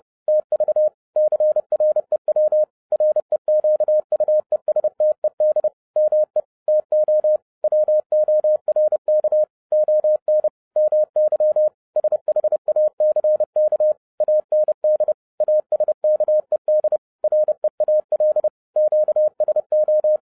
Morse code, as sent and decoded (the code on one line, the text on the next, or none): - ...- / -.-. .-. .-- / .-. --.- ..- . ... - . -.. / -- . / - --- / .-- --- .-. -.- / --- -. / -- -.-- / ... .... .- -.-. -.- / .- -. -.. / .- ... -.- . -.. / .-. .- .-.. / --.- ... ---
TV CREW REQUESTED ME TO WORK ON MY SHACK AND ASKED REAL QSO